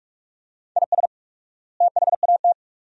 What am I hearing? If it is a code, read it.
Morse code, heard as a keyed tone, about 46 wpm, 700 Hz. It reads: IS THAT
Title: 4x Word-Spacing - Sentences from Top 200 Words - 45wpm